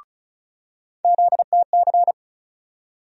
Morse code, heard as a keyed tone, about 36 wpm, 700 Hz. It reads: ZTC
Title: USA Call Sign Postfixes - 3 Letters 35wpm